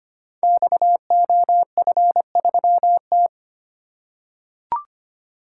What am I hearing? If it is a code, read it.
Morse code, heard as a keyed tone, about 25 wpm, 700 Hz. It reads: XOF3T
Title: Code Groups - 5 Letter:Numbers - 25wpm